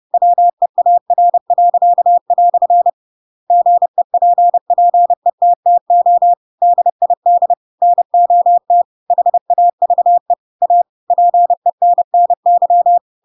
Morse code, heard as a keyed tone, about 30 wpm, 700 Hz. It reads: WEAR." GEPPETTO DID NOT HAVE A PENNY
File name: Pinocchio / Pinocchio ch8 30 30 ik1hge